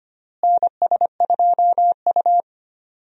Morse code, heard as a keyed tone, about 25 wpm, 700 Hz. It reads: NS2U
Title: USA Call Signs - 4 Characters - Repeated 2x - 25wpm